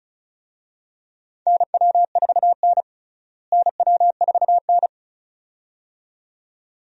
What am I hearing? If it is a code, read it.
Morse code, heard as a keyed tone, about 34 wpm, 700 Hz. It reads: NW4N NW4N
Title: USA Call Signs - 4 Characters - Repeated 2x - 35wpm